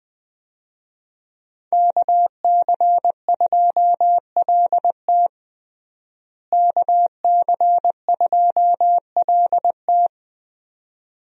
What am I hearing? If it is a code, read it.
Morse code, heard as a keyed tone, about 21 wpm, 700 Hz. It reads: KC2LT KC2LT
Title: USA Call Signs - 5 Characters - Repeated 2x - 20wpm